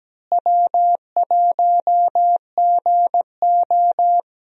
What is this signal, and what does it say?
Morse code, heard as a keyed tone, about 17 wpm, 700 Hz. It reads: W1GO
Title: USA Call Signs - 4 Characters - Repeated 3x - 17wpm